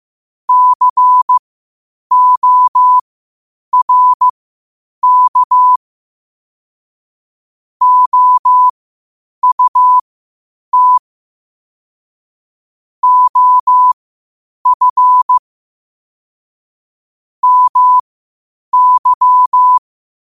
Morse code, heard as a keyed tone, about 15 wpm, spaced as for 8 wpm, 1000 Hz. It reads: CORK OUT OF MY
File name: qotdfiles / 8wpm